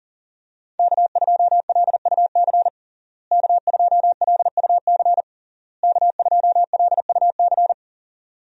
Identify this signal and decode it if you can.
Morse code, heard as a keyed tone, about 40 wpm, 700 Hz. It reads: K2LUC K2LUC K2LUC
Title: USA Call Signs - 5 Characters - Repeated 3x - 40wpm